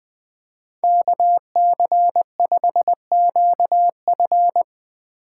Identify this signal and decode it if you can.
Morse code, heard as a keyed tone, about 19 wpm, 700 Hz. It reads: KC5QF
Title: Speed Racing - USA 2x2 Call Signs 17wpm